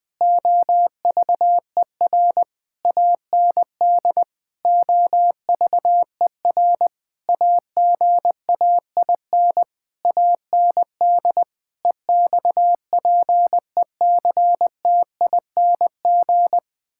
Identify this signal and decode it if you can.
Morse code, heard as a keyed tone, about 20 wpm, 700 Hz. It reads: OVER AND OVER AGAIN AND EXPECTING
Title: Sentences with 8-Word Phrases - 20wpm